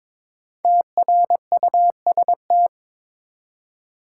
Morse code, heard as a keyed tone, about 22 wpm, 700 Hz. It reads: TRUST
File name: 5 Letter Words - First 3 Letters Given - 22wpm